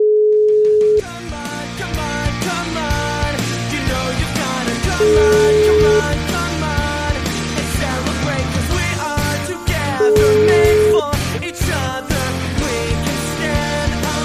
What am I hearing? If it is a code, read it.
Ringback tone (Europe, 425 Hz), with noise.